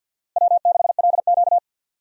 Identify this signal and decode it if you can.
Morse code, heard as a keyed tone, about 51 wpm, 700 Hz. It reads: W6LQ
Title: USA Call Signs - 4 Characters - Repeated 3x - 50wpm